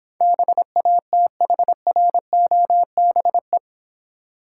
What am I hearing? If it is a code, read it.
Morse code, heard as a keyed tone, about 26 wpm, 700 Hz. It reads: BATHROBE